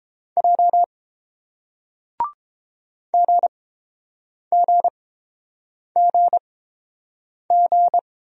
Morse code, heard as a keyed tone, about 29 wpm, spaced as for 7 wpm, 700 Hz. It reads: J GGGG